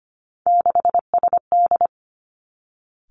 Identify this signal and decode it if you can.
Morse code, heard as a keyed tone, about 25 wpm, 700 Hz. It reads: 6SD
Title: Three Challenging Letter-Number Combinations 25wpm